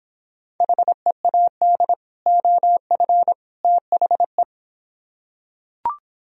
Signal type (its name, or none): Morse code